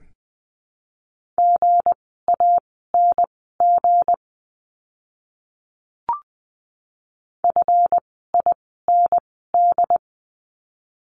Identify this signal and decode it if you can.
Morse code, heard as a keyed tone, about 20 wpm, spaced as for 14 wpm, 700 Hz. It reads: GANG FIND
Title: 160 Lesson 26f - Letter g - Words 20wpm